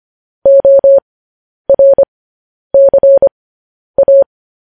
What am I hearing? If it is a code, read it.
Morse code, heard as a keyed tone, about 25 wpm, spaced as for 10 wpm, 550 Hz. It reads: ORCA